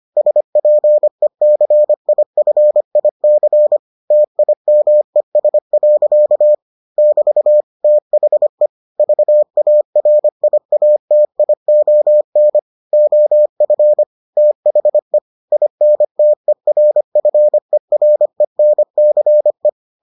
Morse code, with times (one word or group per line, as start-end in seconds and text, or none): 0.2-3.8 SPECIFIC
4.1-6.5 TIMES.
7.0-7.6 <BT>
7.8-8.7 THE
9.0-12.6 VARIATION
12.9-14.0 OF
14.4-15.2 THE
15.5-19.7 INTERFERENCE